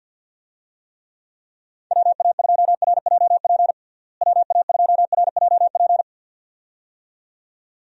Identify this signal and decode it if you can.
Morse code, heard as a keyed tone, about 49 wpm, 700 Hz. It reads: WA2RJP WA2RJP
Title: USA Call Signs - 6 Characters - Repeated 2x - 50wpm